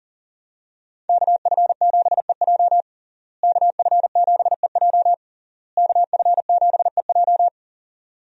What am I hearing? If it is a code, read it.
Morse code, heard as a keyed tone, about 40 wpm, 700 Hz. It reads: KF7EJ KF7EJ KF7EJ
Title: USA Call Signs - 5 Characters - Repeated 3x - 40wpm